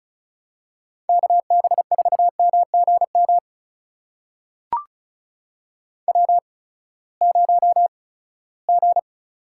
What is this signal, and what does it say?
Morse code, heard as a keyed tone, about 36 wpm, spaced as for 20 wpm, 700 Hz. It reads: KB4MGM W0G
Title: USA Call Signs - 8x Character Spacing - 35wpm